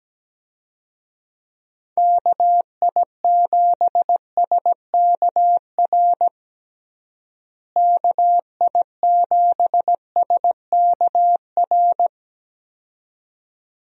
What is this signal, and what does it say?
Morse code, heard as a keyed tone, about 17 wpm, 700 Hz. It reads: KI7SKR KI7SKR
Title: USA Call Signs - 6 Characters - Repeated 2x - 17wpm